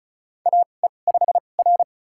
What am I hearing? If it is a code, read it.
Morse code, heard as a keyed tone, about 36 wpm, spaced as for 25 wpm, 700 Hz. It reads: AE5R
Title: USA Call Signs - 4 Characters - 2x Character Spacing - 35wpm